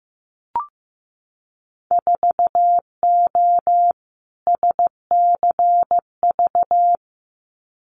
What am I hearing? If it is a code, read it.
Morse code, heard as a keyed tone, about 15 wpm, 700 Hz. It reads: 4O SCV
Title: Winter Field Day Exchanges 15wpm